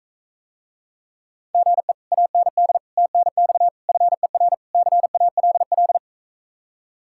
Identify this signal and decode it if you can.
Morse code, heard as a keyed tone, about 45 wpm, 700 Hz. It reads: GE AND TNX FER CALL